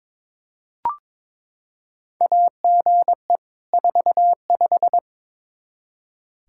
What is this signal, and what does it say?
Morse code, heard as a keyed tone, about 23 wpm, 700 Hz. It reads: AGE 45
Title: Practice QSO - Exchange 3 - More Info - 22wpm